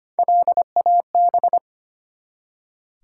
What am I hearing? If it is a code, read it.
Morse code, heard as a keyed tone, about 25 wpm, 700 Hz. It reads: LAB
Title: Three Letter Words - 25wpm